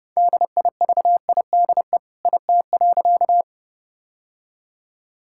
Morse code, heard as a keyed tone, about 30 wpm, 700 Hz. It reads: DIVIDE IT.
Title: Sentences from Top 1600 Words 30wpm